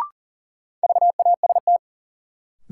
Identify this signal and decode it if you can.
Morse code, heard as a keyed tone, about 40 wpm, 700 Hz. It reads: VAST
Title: Sentences from Top 1300 Words - New Word Drill - 40wpm